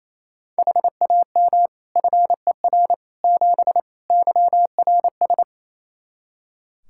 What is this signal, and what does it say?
Morse code, heard as a keyed tone, about 28 wpm, 700 Hz. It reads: HAM FER 7 YRS
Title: Practice QSO - Exchange 3 - More Info - 28wpm